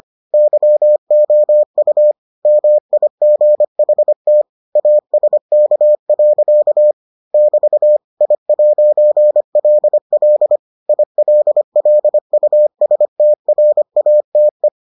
Morse code, heard as a keyed tone, about 25 wpm, 600 Hz. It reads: YOU MIGHT ASK. <BT> I'LL ILLUSTRATE